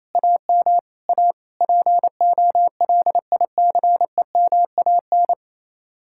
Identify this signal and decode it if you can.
Morse code, heard as a keyed tone, about 37 wpm, 700 Hz. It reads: AM A POLICEMAN